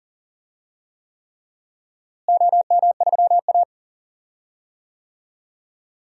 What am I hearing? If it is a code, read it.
Morse code, heard as a keyed tone, about 40 wpm, 700 Hz. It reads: OM3A
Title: DXCC-100 Call Signs - 4 Characters - Repeated 3x - 40wpm